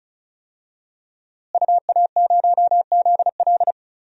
Morse code, heard as a keyed tone, about 36 wpm, 700 Hz. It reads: UA0ZL
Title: DXCC-100 Call Signs - 5 Characters - Repeated 3x - 35wpm